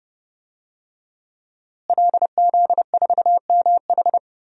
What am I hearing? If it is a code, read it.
Morse code, heard as a keyed tone, about 30 wpm, 700 Hz. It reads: LZ4MH
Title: DXCC-100 Call Signs - 5 Characters - Repeated 3x - 30wpm